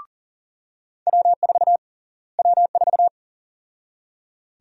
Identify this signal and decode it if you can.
Morse code, heard as a keyed tone, about 40 wpm, 700 Hz. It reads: W4 W4
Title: USA Call Sign Prefixes - Repeated 2x - 40wpm